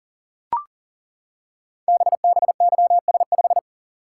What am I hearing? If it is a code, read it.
Morse code, heard as a keyed tone, about 42 wpm, 700 Hz. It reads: BBYS5
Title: Code Groups - 5 Letter:Numbers - 40wpm